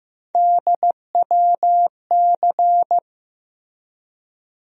Morse code, heard as a keyed tone, about 15 wpm, 700 Hz. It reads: DWC
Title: USA Call Sign Postfixes - 3 Letters 15wpm